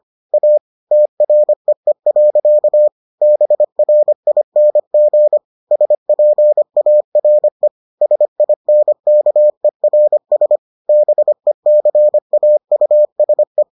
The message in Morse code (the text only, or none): A TREE. BRING SPARE SINKERS BECAUSE